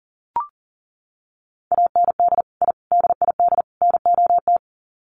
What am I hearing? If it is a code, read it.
Morse code, heard as a keyed tone, about 40 wpm, 700 Hz. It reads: AND I DID NOT
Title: Sets of 4 Words 40wpm